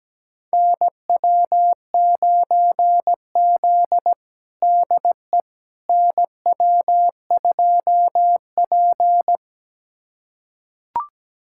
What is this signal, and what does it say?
Morse code, heard as a keyed tone, about 17 wpm, 700 Hz. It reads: NW9Z DE NW2P
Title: USA Callsign DE Callsign - 4 Characters - 17wpm